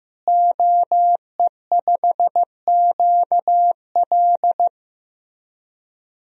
Morse code, heard as a keyed tone, about 15 wpm, 700 Hz. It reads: OE5QL
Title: DXCC-100 Call Signs - 5 Characters - Repeated 2x - 15wpm